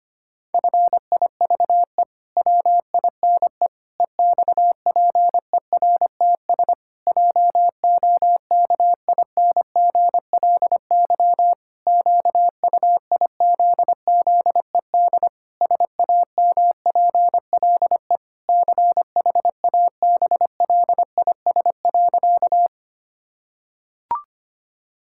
Morse code, heard as a keyed tone, about 25 wpm, 700 Hz. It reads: FIVE WINE EXPERTS JOKINGLY QUIZZED SAMPLE CHABLIS.
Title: Pangram Challenge v2 - 25wpm